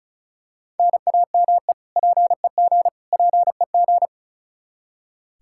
Morse code, heard as a keyed tone, about 34 wpm, 700 Hz. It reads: NAME PEG PEG
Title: Practice QSO - Exchange 1 - Beginning - 35wpm